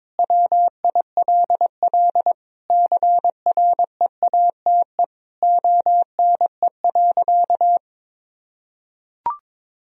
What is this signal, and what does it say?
Morse code, heard as a keyed tone, about 22 wpm, 700 Hz. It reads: WILL CREATE ONE.
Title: Sentences from Top 1000 Words - Review Part 3 - 22wpm